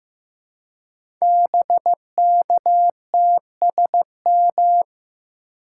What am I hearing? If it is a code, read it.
Morse code, heard as a keyed tone, about 15 wpm, 700 Hz. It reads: BKTSM
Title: Code Groups - 5 Letters - 15wpm